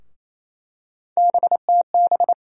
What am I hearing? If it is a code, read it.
Morse code, heard as a keyed tone, about 28 wpm, 700 Hz. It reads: BTB
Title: USA Call Sign Postfixes - 3 Letters 28wpm